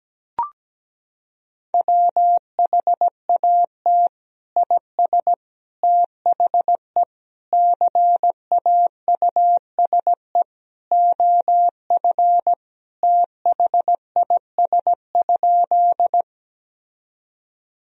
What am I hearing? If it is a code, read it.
Morse code, heard as a keyed tone, about 17 wpm, 700 Hz. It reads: WHAT IS THE CAUSE OF THIS?
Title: Sentences from Top 1000 Words - Review Part 1 - 17wpm